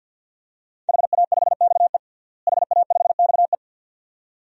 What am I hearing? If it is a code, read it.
Morse code, heard as a keyed tone, about 47 wpm, 700 Hz. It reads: HA5XE HA5XE
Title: DXCC-100 Call Signs - 5 Characters - Repeated 2x - 50wpm